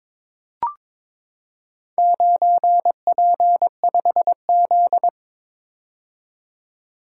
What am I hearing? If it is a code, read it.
Morse code, heard as a keyed tone, about 22 wpm, 700 Hz. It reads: 9P5Z
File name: practice / 22wpm / Four Letter-Number Combinations - 22wpm